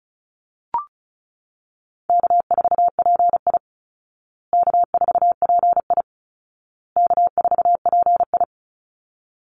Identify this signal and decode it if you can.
Morse code, heard as a keyed tone, about 34 wpm, 700 Hz. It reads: K4PI K4PI K4PI
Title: Easy Call Signs Repeated 3x - 35wpm